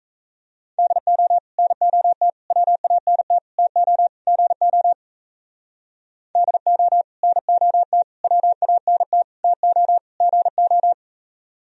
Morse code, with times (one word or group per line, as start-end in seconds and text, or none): 0.8-1.4 DO
1.6-2.3 NOT
2.5-3.4 WANT
3.6-4.1 TO
4.3-4.9 GO
6.4-7.0 DO
7.2-8.0 NOT
8.2-9.2 WANT
9.4-10.0 TO
10.2-10.9 GO